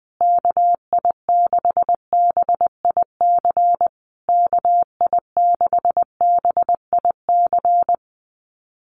Morse code, heard as a keyed tone, about 20 wpm, 700 Hz. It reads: KI6BIC KI6BIC
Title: USA Call Signs Repeated 3x 20wpm